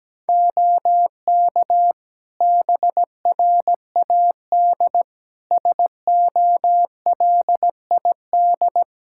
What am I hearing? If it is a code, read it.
Morse code, heard as a keyed tone, about 17 wpm, 700 Hz. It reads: OK BRAD SOLID
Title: Practice QSO - Exchange 3 - More Info - 17wpm